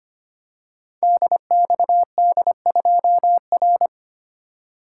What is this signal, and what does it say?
Morse code, heard as a keyed tone, about 25 wpm, 690 Hz. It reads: DXD2R
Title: Code Groups - 5 Letter:Numbers - 25wpm